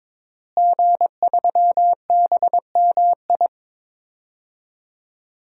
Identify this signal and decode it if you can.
Morse code, heard as a keyed tone, about 22 wpm, 700 Hz. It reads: G3BMI
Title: DXCC-100 Call Signs - 5 Characters - Repeated 2x - 22wpm